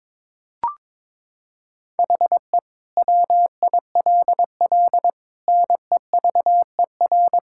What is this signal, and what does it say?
Morse code, heard as a keyed tone, about 22 wpm, 700 Hz. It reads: HE WILL NEVER
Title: Sentences from Top 1000 Words - Review Part 6 - 22wpm